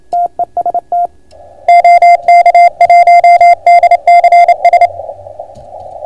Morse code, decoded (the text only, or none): TEST OK1DCS